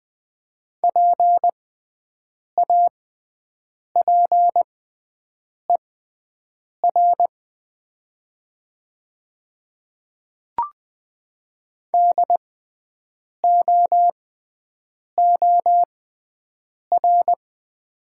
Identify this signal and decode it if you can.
Morse code, heard as a keyed tone, about 20 wpm, spaced as for 7 wpm, 700 Hz. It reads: PAPER DOOR
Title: 6x Character Spacing - Top 300 Words - 20wpm